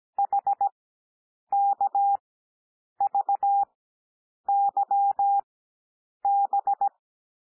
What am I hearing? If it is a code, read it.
Morse code, heard as a keyed tone, about 17 wpm, spaced as for 8 wpm, 800 Hz. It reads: HKVYB
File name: cwmp3 / L2 SOUND LETTERS 17 wpm